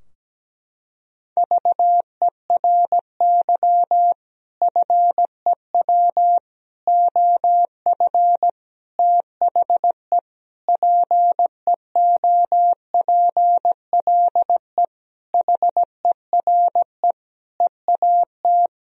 Morse code, heard as a keyed tone, about 17 wpm, 700 Hz. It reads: VERY FEW OF THE PEOPLE HERE EAT
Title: Sentences from Top 1000 Words - Review Part 3 - 17wpm